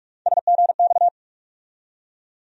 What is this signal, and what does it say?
Morse code, heard as a keyed tone, about 46 wpm, 700 Hz. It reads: SGX
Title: USA Call Sign Postfixes - 3 Letters 45wpm